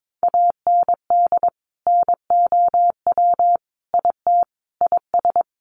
Morse code, heard as a keyed tone, about 22 wpm, 700 Hz. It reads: AND NOW IT IS
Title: Sentences from Top 1000 Words 22wpm